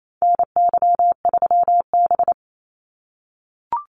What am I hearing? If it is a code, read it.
Morse code, heard as a keyed tone, about 28 wpm, 700 Hz. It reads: NY3B
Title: Call Signs - 28wpm